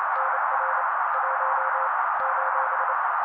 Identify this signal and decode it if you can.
Morse code, heard as a keyed tone, about 28 wpm, 530 Hz. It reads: NR 1 :
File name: daily / 27wpm noise